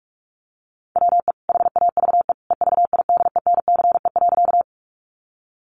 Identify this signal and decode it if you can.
Morse code, heard as a keyed tone, about 44 wpm, 700 Hz. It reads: WE HAVE EVIDENCE.
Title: Sentences from Top 1300 Words 45wpm